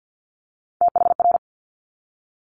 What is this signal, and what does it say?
Morse code, heard as a keyed tone, about 50 wpm, 700 Hz. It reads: THR
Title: Top Three Letter Combinations in Words 50wpm